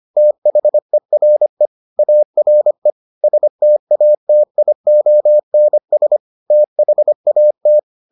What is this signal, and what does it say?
Morse code, heard as a keyed tone, about 25 wpm, 600 Hz. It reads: THERE ARE STATIONS THAT